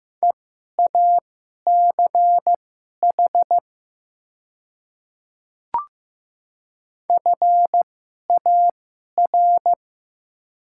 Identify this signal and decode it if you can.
Morse code, heard as a keyed tone, about 15 wpm, spaced as for 11 wpm, 700 Hz. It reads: EACH FAR
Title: 2x Character Spacing - Top 300 Words - 15wpm